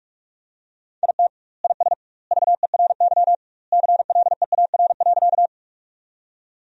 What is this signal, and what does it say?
Morse code, heard as a keyed tone, about 47 wpm, 700 Hz. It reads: IT IS VERY CLEAR.
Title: 2x Word-Spacing - Sentences from Top 500 Words - 45wpm